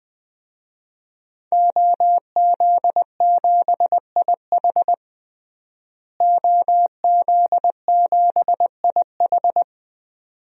Morse code, heard as a keyed tone, about 20 wpm, 700 Hz. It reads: OZ7IH OZ7IH
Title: DXCC-100 Call Signs - 5 Characters - Repeated 2x - 20wpm